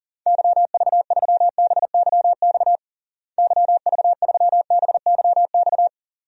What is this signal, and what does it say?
Morse code, heard as a keyed tone, about 37 wpm, 700 Hz. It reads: YV3BYX YV3BYX
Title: DXCC-100 Call Signs - 6 Characters - Repeated 2x - 40wpm